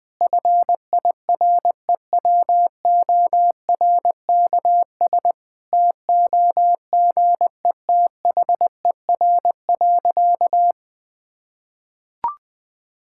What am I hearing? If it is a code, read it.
Morse code, heard as a keyed tone, about 20 wpm, 700 Hz. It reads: FIREWORKS TOGETHER.
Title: Difficult Sentences with Hints - 20wpm